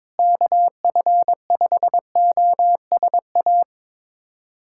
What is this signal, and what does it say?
Morse code, heard as a keyed tone, about 22 wpm, 700 Hz. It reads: KF5OSA
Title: USA Callsign DE Callsign - 6 Characters - 22wpm